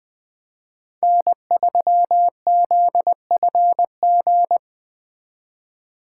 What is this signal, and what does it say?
Morse code, heard as a keyed tone, about 21 wpm, 700 Hz. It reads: N3ZFG